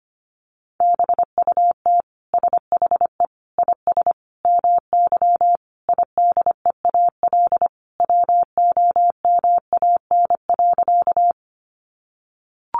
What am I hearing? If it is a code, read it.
Morse code, heard as a keyed tone, about 25 wpm, 700 Hz. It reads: BUT SHE IS MY IDEAL WOMAN.
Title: Sentences from Top 1400 Words 25wpm